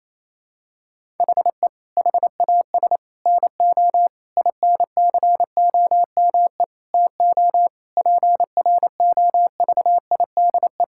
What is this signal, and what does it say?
Morse code, heard as a keyed tone, about 28 wpm, 700 Hz. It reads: HE HAS NO INCOME TO PROVIDE
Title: Sentences from Top 1000 Words - Review Part 5 - 28wpm